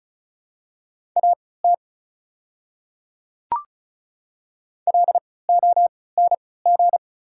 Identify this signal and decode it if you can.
Morse code, heard as a keyed tone, about 36 wpm, spaced as for 20 wpm, 700 Hz. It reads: AT LONG
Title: 3x Character Spacing - Top 200 Words - 35wpm